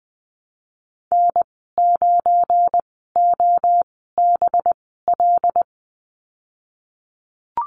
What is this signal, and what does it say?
Morse code, heard as a keyed tone, about 20 wpm, spaced as for 15 wpm, 700 Hz. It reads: N9OBL
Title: 193 Lesson 31g - Letter v - Call Signs 20wpm